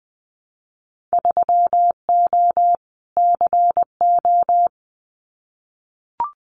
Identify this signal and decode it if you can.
Morse code, heard as a keyed tone, about 20 wpm, 700 Hz. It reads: 3O CO
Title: Winter Field Day Exchanges Encore 20wpm